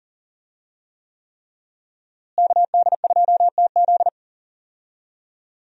Morse code, heard as a keyed tone, about 40 wpm, 700 Hz. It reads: KD2TZ
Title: USA Call Signs - 5 Characters - Repeated 3x - 40wpm